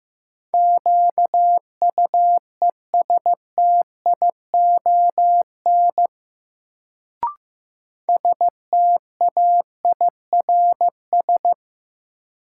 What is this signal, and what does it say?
Morse code, heard as a keyed tone, about 15 wpm, 700 Hz. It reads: QUESTION STAIRS
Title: Sets of 5 Words Encore - New Word Drill - 15wpm